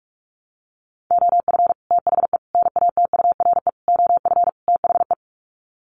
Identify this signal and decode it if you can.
Morse code, heard as a keyed tone, about 44 wpm, 700 Hz. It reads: OF THE NATURE OF THE
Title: Sets of 5 Words Encore2 45wpm